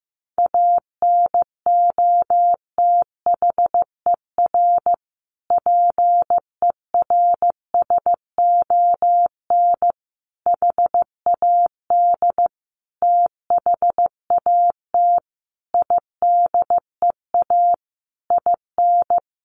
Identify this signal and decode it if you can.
Morse code, heard as a keyed tone, about 15 wpm, 700 Hz. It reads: ANOTHER PERSON HAD THAT IDEA IN
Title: Sentences from Top 1400 Words 15wpm